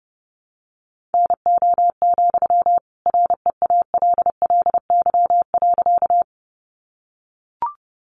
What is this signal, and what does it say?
Morse code, heard as a keyed tone, about 28 wpm, 700 Hz. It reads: NO, REALLY.
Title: Sentences from Top 600 Words 30wpm